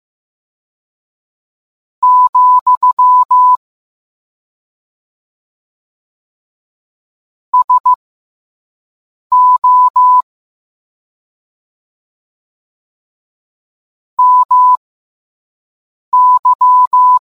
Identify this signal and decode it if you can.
Morse code, heard as a keyed tone, about 15 wpm, spaced as for 5 wpm, 1000 Hz. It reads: , SO MY